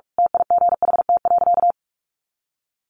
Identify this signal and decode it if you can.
Morse code, heard as a keyed tone, about 44 wpm, 700 Hz. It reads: TIGHT.
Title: A Sound Of Thunder 45wpm